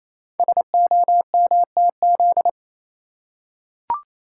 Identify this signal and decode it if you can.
Morse code, heard as a keyed tone, about 29 wpm, 700 Hz. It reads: SOMTZ
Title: Code Groups - 5 Letter:Numbers - 28wpm